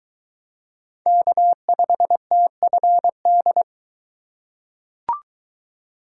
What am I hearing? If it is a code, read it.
Morse code, heard as a keyed tone, about 23 wpm, 700 Hz. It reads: K5TFD